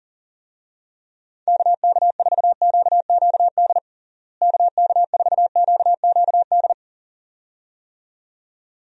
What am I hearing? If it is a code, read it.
Morse code, heard as a keyed tone, about 40 wpm, 700 Hz. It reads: KK4QQD KK4QQD
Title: USA Call Signs - 6 Characters - Repeated 2x - 40wpm